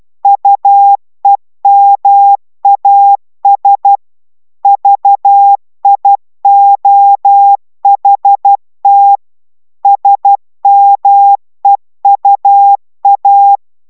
Morse code, heard as a keyed tone, about 12 wpm, 800 Hz. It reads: UEMAS VIOHT SMEUA
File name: taviro tanulas / cw 69 05 10